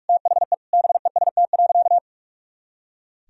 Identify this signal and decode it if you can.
Morse code, heard as a keyed tone, about 46 wpm, 690 Hz. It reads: THE BEST.
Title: Sentences from Top 500 Words and Abbreviations 45wpm